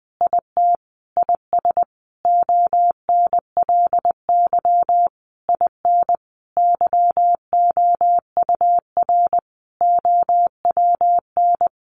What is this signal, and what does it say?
Morse code, heard as a keyed tone, about 20 wpm, 700 Hz. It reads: IT IS ONLY IN YOUR OWN
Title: Sentences from Top 1300 Words 20wpm